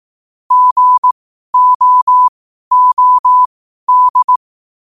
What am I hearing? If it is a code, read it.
Morse code, heard as a keyed tone, about 18 wpm, spaced as for 12 wpm, 1000 Hz. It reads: GOOD
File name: qotdfiles / 12wpm